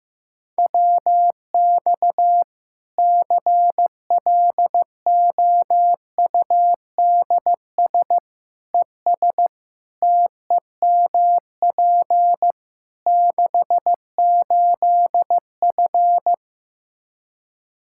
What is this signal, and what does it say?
Morse code, heard as a keyed tone, about 15 wpm, 700 Hz. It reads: WX CLOUDS ES TEMP 68F
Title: Practice QSO - Exchange 2 - Info - 15wpm